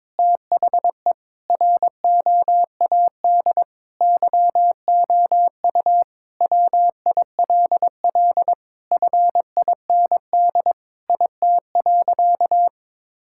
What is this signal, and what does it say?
Morse code, heard as a keyed tone, about 22 wpm, 700 Hz. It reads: THE ROAD YOU WILL FIND IT.
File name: practice / 22wpm / Sentences with 3-Word Phrases - 22wpm